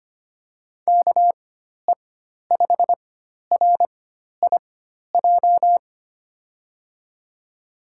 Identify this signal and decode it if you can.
Morse code, heard as a keyed tone, about 25 wpm, spaced as for 12 wpm, 700 Hz. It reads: KE5RIJ
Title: USA Call Signs - 4x Character Spacing - 25wpm